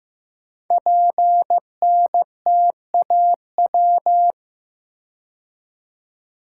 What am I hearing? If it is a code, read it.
Morse code, heard as a keyed tone, about 15 wpm, 700 Hz. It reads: PNTAW
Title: Code Groups - 5 Letters - 15wpm